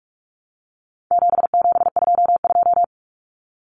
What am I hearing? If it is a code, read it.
Morse code, heard as a keyed tone, about 44 wpm, 700 Hz. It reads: 7722
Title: Numbers - 4-Digit 45wpm